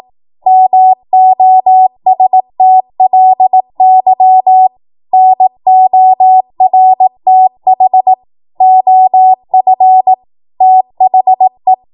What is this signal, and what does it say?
Morse code, heard as a keyed tone, about 18 wpm, 750 Hz. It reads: MOSTLY NORTH OF THE
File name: Morse / 18WPM